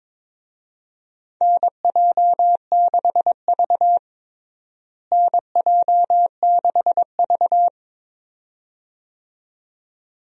Morse code, heard as a keyed tone, about 22 wpm, 700 Hz. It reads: NJ6V NJ6V
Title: USA Call Signs - 4 Characters - Repeated 2x - 22wpm